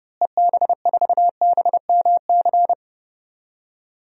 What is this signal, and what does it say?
Morse code, heard as a keyed tone, about 29 wpm, 700 Hz. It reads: EB4BMC
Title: DXCC-100 Call Signs - 6 Characters - Repeated 3x - 30wpm